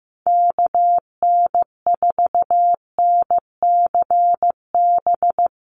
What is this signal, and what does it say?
Morse code, heard as a keyed tone, about 15 wpm, 700 Hz. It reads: KN4NCB
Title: USA Call Signs Repeated 3x 15wpm